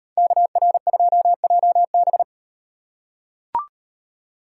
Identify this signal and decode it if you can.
Morse code, heard as a keyed tone, about 38 wpm, 700 Hz. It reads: KR2JB